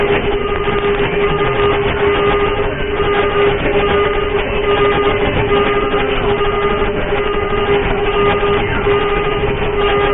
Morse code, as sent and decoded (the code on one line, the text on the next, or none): none